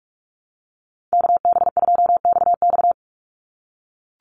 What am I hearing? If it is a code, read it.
Morse code, heard as a keyed tone, about 44 wpm, 700 Hz. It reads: KB2XX
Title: NA SOTA Chaser Callsigns 45wpm